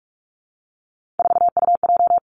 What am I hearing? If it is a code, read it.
Morse code, heard as a keyed tone, about 46 wpm, 700 Hz. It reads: 4UJ
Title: Three Challenging Letter-Number Combinations 45wpm